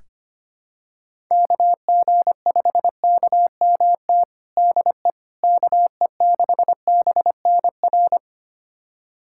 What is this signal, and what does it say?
Morse code, heard as a keyed tone, about 25 wpm, 700 Hz. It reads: KG5KMT DE KE6BNR
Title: USA Callsign DE Callsign - 6 Characters - 25wpm